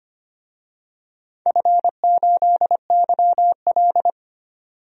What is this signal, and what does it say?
Morse code, heard as a keyed tone, about 25 wpm, 700 Hz. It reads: F8YL
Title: DXCC-100 Call Signs - 4 Characters - 25wpm